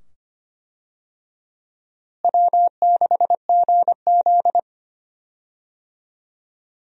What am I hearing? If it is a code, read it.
Morse code, heard as a keyed tone, about 25 wpm, 700 Hz. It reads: W6GZ
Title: USA Call Signs - 4 Characters - Repeated 2x - 25wpm